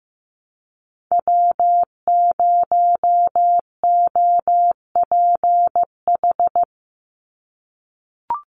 Morse code, heard as a keyed tone, about 15 wpm, 700 Hz. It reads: W0OPH
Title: USA Call Signs 15wpm